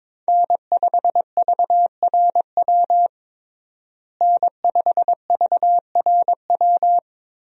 Morse code, heard as a keyed tone, about 22 wpm, 700 Hz. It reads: N5VRW N5VRW
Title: USA Call Signs - 5 Characters - Repeated 3x - 22wpm